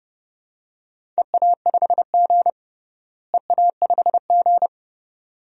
Morse code, heard as a keyed tone, about 32 wpm, 700 Hz. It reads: EA5G EA5G
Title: DXCC-100 Call Signs - 4 Characters - Repeated 2x - 30wpm